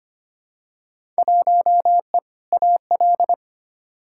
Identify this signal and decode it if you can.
Morse code, heard as a keyed tone, about 25 wpm, 700 Hz. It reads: 1E AL
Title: Field Day Exchanges - 25wpm